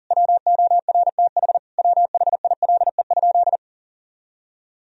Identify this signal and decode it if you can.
Morse code, heard as a keyed tone, about 40 wpm, 700 Hz. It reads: WORTH WHILE?
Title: Sentences from Top 1000 Words - Review Part 1 - 40wpm